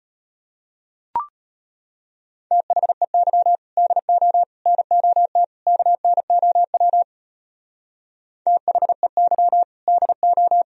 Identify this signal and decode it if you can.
Morse code, heard as a keyed tone, about 31 wpm, 700 Hz. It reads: THEY DO NOT KNOW THEY DO
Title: Speed-Racing - Sets of 4 Words 25wpm